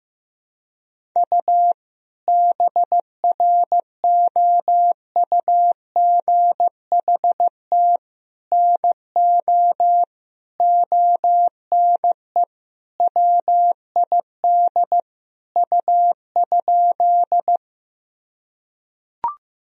Morse code, heard as a keyed tone, about 15 wpm, 700 Hz. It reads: U BROUGHT NO ONE WID U?
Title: Sentences from Top 500 Words and Abbreviations 15wpm